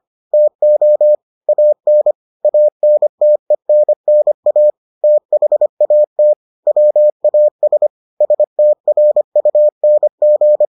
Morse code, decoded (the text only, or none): TO AN ANTENNA THAT WAS STRUNG